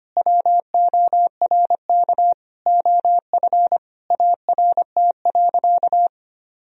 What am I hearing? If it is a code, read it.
Morse code, heard as a keyed tone, about 25 wpm, 700 Hz. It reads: WORK OF ART.
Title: Sentences from Top 1000 Words - Review Part 3 - 25wpm